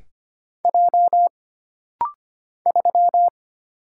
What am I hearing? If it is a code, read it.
Morse code, heard as a keyed tone, about 25 wpm, 700 Hz. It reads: J 3
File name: Single Character (Letter:Number:Punctuation:Prosign) - Rapid-Fire - 25wpm